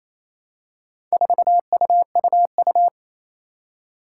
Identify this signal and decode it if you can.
Morse code, heard as a keyed tone, about 28 wpm, 700 Hz. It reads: 4UUU
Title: Four Challenging Letter-Number Combinations 28wpm